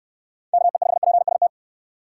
Morse code, heard as a keyed tone, about 68 wpm, 700 Hz. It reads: KE5PST